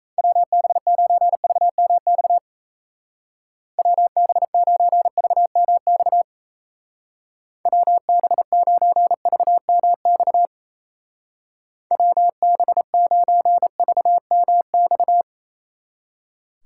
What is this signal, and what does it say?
Morse code, heard as a keyed tone, about 35 wpm, 700 Hz. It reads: WB9VMX WB9VMX WB9VMX WB9VMX
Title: Speed Racing - USA 2x3 Call Signs 28wpm